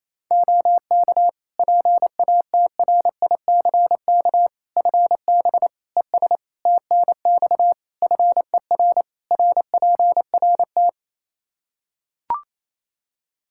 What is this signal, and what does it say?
Morse code, heard as a keyed tone, about 28 wpm, 700 Hz. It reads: OK PATRICK FB ES TNX FER RPRT
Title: Practice QSO - Exchange 2 - Info - 28wpm